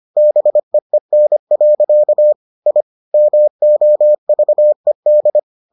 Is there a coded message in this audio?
Yes: Morse code from 0.2 to 5.4 s, heard as a keyed tone, about 25 wpm, 600 Hz. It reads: BEEN. I MOVED